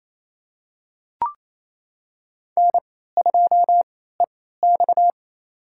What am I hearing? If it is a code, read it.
Morse code, heard as a keyed tone, about 28 wpm, spaced as for 16 wpm, 700 Hz. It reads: N2EX
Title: USA Call Signs - 4 Characters - 3x Character Spacing - 28wpm